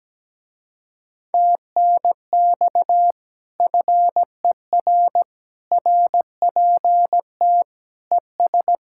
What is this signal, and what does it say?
Morse code, heard as a keyed tone, about 17 wpm, 700 Hz. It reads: TNX FER RPT ES
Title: Speed-Racing Common QSO Phrases - 17wpm